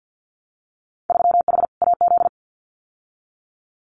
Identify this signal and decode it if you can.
Morse code, heard as a keyed tone, about 50 wpm, 700 Hz. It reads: 3H AZ
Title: Winter Field Day Exchanges Encore 50wpm